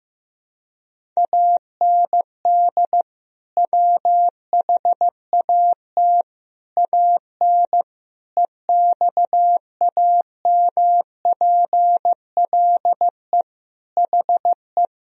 Morse code, heard as a keyed tone, about 15 wpm, 700 Hz. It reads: AND WHAT AN EXAMPLE HE
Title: Sentences from Top 1000 Words - Review Part 1 - 15wpm